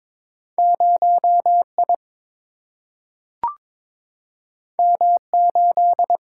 Morse code, heard as a keyed tone, about 22 wpm, 700 Hz. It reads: 0I M8
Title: Single-Letter Single-Number 22wpm